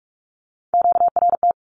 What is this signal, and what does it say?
Morse code, heard as a keyed tone, about 46 wpm, 700 Hz. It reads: QRT
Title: Q-Codes with Spoken Meaning - 45wpm